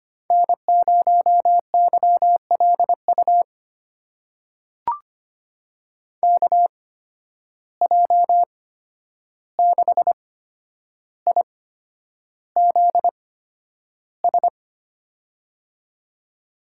Morse code, heard as a keyed tone, about 25 wpm, spaced as for 10 wpm, 700 Hz. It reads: N0YLU KJ6IZS
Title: USA Call Signs - 8x Character Spacing - 25wpm